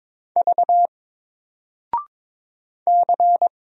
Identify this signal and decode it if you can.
Morse code, heard as a keyed tone, about 22 wpm, 700 Hz. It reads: V C
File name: Single-Letter-Number-22wpm